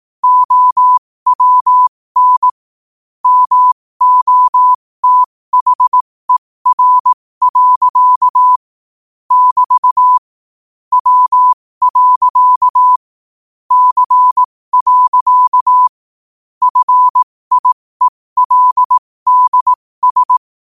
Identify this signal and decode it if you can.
Morse code, heard as a keyed tone, about 18 wpm, spaced as for 15 wpm, 1000 Hz. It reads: OWN MOTHER. <BT> W. C. FIELDS
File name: qotdfiles / 15wpm